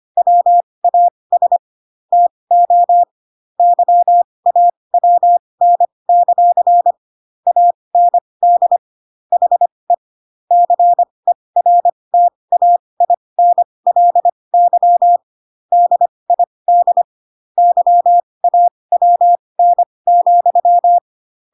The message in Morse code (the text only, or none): WAS TO YAWN; AND HE CERTAINLY DID YAWN,